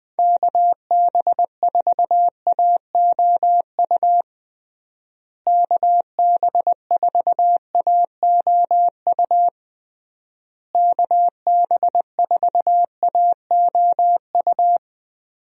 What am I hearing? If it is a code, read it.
Morse code, heard as a keyed tone, about 21 wpm, 700 Hz. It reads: KB4AOU KB4AOU KB4AOU
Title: USA Call Signs - 6 Characters - Repeated 3x - 20wpm